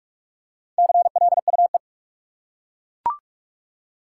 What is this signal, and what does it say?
Morse code, heard as a keyed tone, about 42 wpm, 700 Hz. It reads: CLUE
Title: Four Letter Words - 45wpm